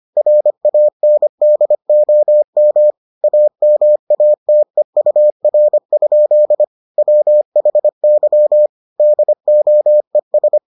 Morse code, heard as a keyed tone, about 25 wpm, 600 Hz. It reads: RANDOM AMATEUR? WHY DOES